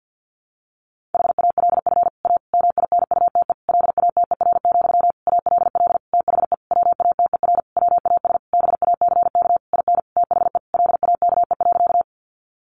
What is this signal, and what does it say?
Morse code, heard as a keyed tone, about 50 wpm, 700 Hz. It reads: HALF A MINUTE LATER, ALL THE WATER WAS BACK IN THE LAKE.
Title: Sentences from Top 1100 Words 50wpm